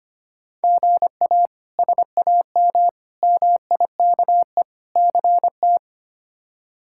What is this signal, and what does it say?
Morse code, heard as a keyed tone, about 25 wpm, 700 Hz. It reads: GA SAM MIKE CT
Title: K1USN SST QSO Practice - 25wpm